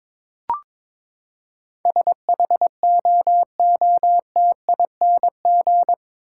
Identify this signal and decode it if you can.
Morse code, heard as a keyed tone, about 21 wpm, 700 Hz. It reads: SHOOTING